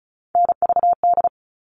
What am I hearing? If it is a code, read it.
Morse code, heard as a keyed tone, about 34 wpm, 700 Hz. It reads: NVD
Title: All Cut Numbers - 3-Digit 35wpm